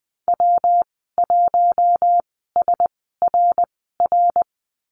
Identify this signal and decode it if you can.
Morse code, heard as a keyed tone, about 20 wpm, spaced as for 15 wpm, 700 Hz. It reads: W1SRR
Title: 147 Lesson 24g - Letter y - Call Signs 20wpm